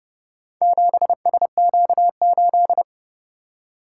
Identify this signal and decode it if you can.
Morse code, heard as a keyed tone, about 29 wpm, 700 Hz. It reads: 7SQ8